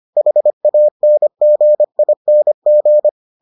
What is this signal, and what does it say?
Morse code, heard as a keyed tone, about 25 wpm, 600 Hz. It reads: HANGING